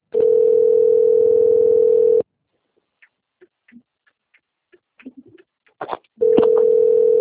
Ringback tone (North America), 440 + 480 Hz.
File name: rg-1600-unknown- - - .13507